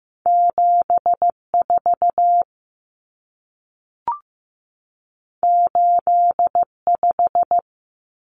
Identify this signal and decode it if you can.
Morse code, heard as a keyed tone, about 15 wpm, 700 Hz. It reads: 74 85
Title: Numbers - 2-Digit 15wpm